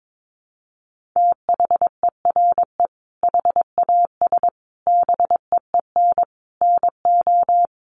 Morse code, heard as a keyed tone, about 22 wpm, 700 Hz. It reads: THERE HAS BEEN NO
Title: Sentences from Top 1500 Words 22wpm